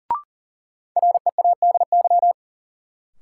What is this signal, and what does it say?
Morse code, heard as a keyed tone, about 40 wpm, 700 Hz. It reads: READY
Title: Sentences from Top 400 Words - New Word Drill - 40wpm